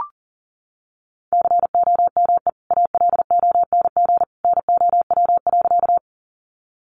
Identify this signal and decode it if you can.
Morse code, heard as a keyed tone, about 40 wpm, 700 Hz. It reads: COME ALONG NOW.
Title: Sentences from Top 700 Words 40wpm